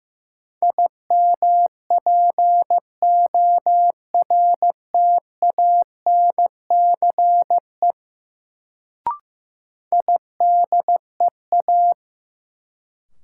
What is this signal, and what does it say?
Morse code, heard as a keyed tone, about 15 wpm, 700 Hz. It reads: IMPORTANCE IDEA
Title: Sets of 3 Words - New Word Drill - 15wpm